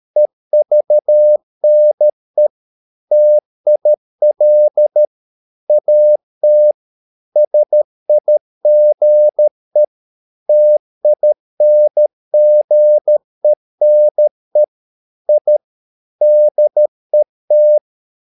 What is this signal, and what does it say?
Morse code, heard as a keyed tone, about 13 wpm, 600 Hz. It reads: EVNE TIL AT SIGE TINGENE I DET
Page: Ferie 13 wpm